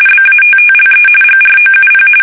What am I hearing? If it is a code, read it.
RTTY, about 45.45 baud. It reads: BATTLE GROUP
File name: fsk850 / battleshipnj